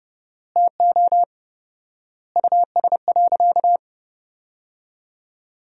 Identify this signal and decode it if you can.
Morse code, heard as a keyed tone, about 32 wpm, 700 Hz. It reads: TO US.
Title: 4x Word-Spacing - Sentences from Top 500 Words - 30wpm